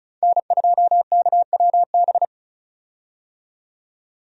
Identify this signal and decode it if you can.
Morse code, heard as a keyed tone, about 36 wpm, 700 Hz. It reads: N2KWB
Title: USA Call Signs - 5 Characters - 35wpm